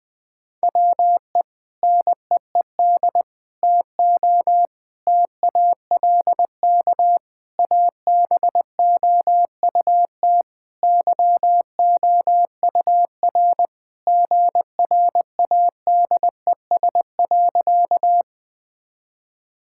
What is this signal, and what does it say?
Morse code, heard as a keyed tone, about 20 wpm, 700 Hz. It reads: WE NEED TO TALK ABOUT YOUR GRADES.
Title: Sentences with 3-Word Phrases - Spoken Beginning - 20wpm